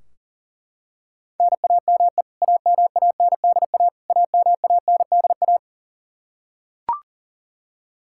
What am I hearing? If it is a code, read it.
Morse code, heard as a keyed tone, about 40 wpm, 700 Hz. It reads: NAME AMANDA AMANDA
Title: K1USN SST QSO Practice 40wpm